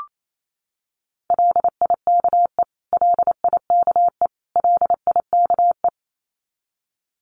Morse code, heard as a keyed tone, about 28 wpm, 700 Hz. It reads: LIKE LIKE LIKE
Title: Top 300 Words Repeated 28wpm